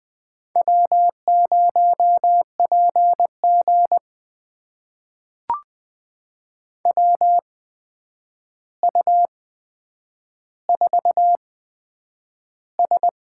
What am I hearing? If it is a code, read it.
Morse code, heard as a keyed tone, about 20 wpm, spaced as for 9 wpm, 700 Hz. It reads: W0PG WU4S
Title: USA Call Signs - 8x Character Spacing - 20wpm